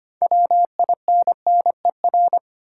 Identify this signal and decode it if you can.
Morse code, heard as a keyed tone, about 25 wpm, 700 Hz. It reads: WINNER
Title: Six Letter Words - 25wpm